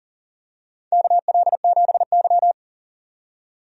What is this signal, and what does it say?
Morse code, heard as a keyed tone, about 40 wpm, 700 Hz. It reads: KL7Y
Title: DXCC-100 Call Signs - 4 Characters - 40wpm